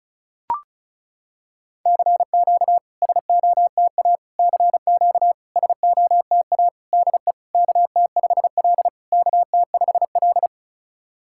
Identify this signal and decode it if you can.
Morse code, heard as a keyed tone, about 36 wpm, 690 Hz. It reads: CQ SOTA CQ SOTA DE KT5L KT5L
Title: SOTA QSO Practice 35wpm